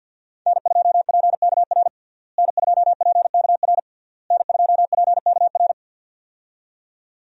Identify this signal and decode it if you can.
Morse code, heard as a keyed tone, about 48 wpm, 700 Hz. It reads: N2PKR N2PKR N2PKR
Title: USA Call Signs - 5 Characters - Repeated 3x - 50wpm